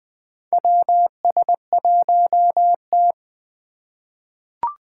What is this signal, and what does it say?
Morse code, heard as a keyed tone, about 20 wpm, 700 Hz. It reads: WS1T
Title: USA Call Signs - 3x Character Spacing - 20wpm